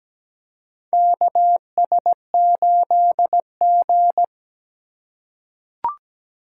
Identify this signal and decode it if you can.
Morse code, heard as a keyed tone, about 17 wpm, 700 Hz. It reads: KS8G
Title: USA Call Signs - 4 Characters - 8x Character Spacing - 17wpm